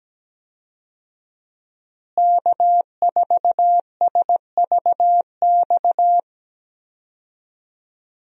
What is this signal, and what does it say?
Morse code, heard as a keyed tone, about 17 wpm, 700 Hz. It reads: K4SVX
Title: USA Call Signs - 5 Characters - Repeated 3x - 17wpm